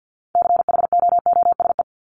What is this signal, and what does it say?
Morse code, heard as a keyed tone, about 50 wpm, 700 Hz. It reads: CHOOSE
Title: Top 501-750 Words 50wpm